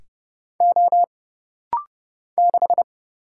Morse code, heard as a keyed tone, about 30 wpm, 700 Hz. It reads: O 6
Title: Single Letter-Number Alt - Rapid-Fire 30wpm